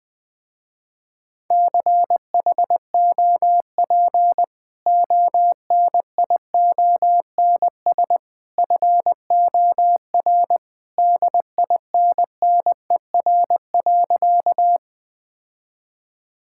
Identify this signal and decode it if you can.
Morse code, heard as a keyed tone, about 19 wpm, 700 Hz. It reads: CHOP ONIONS FOR DINNER.
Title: Easy Sentences with Hints - 20wpm